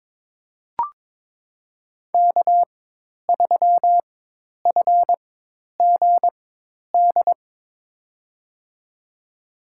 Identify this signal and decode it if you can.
Morse code, heard as a keyed tone, about 22 wpm, spaced as for 10 wpm, 700 Hz. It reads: K3FGD
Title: USA Call Signs - 5 Characters - 4x Character Spacing - 22wpm